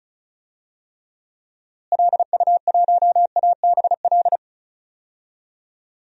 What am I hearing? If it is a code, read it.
Morse code, heard as a keyed tone, about 36 wpm, 700 Hz. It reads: LU1ABL
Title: DXCC-100 Call Signs - 6 Characters - Repeated 3x - 35wpm